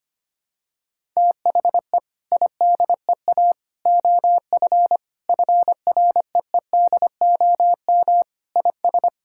Morse code, heard as a keyed tone, about 25 wpm, 700 Hz. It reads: THE IDEA OF FREEDOM IS
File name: practice / 25wpm / Sentences with 3-Word Phrases - 25wpm